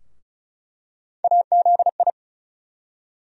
Morse code, heard as a keyed tone, about 34 wpm, 700 Hz. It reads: AZI
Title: USA Call Sign Postfixes - 3 Letters 35wpm